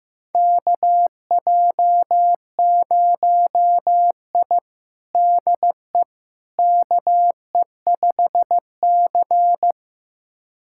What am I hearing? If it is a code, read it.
Morse code, heard as a keyed tone, about 16 wpm, 700 Hz. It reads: KJ0I DE KE5C
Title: USA Callsign DE Callsign - 4 Characters - 15wpm